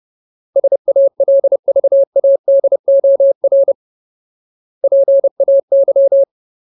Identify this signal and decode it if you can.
Morse code, heard as a keyed tone, about 30 wpm, 550 Hz. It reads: SALVADOR PAY
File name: Ultimate Word List / 30WPM Ultimate